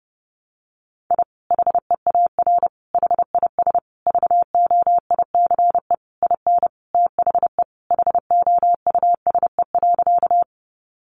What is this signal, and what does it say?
Morse code, heard as a keyed tone, about 30 wpm, 700 Hz. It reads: I HEAR HIS VOICE IN THE HOUSE.
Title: Sentences from Top 500 Words 30wpm